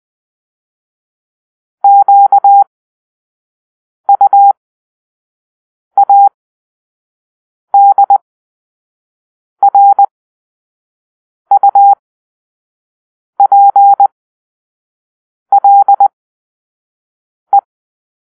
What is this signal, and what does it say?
Morse code, heard as a keyed tone, about 20 wpm, spaced as for 5 wpm, 800 Hz. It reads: QUADRUPLE
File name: Grp-5 Drill-2